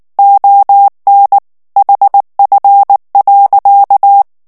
Morse code, heard as a keyed tone, about 19 wpm, 800 Hz. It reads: ON HF.
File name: ANGLAIS texte clair 3 vitesse 18 mots minute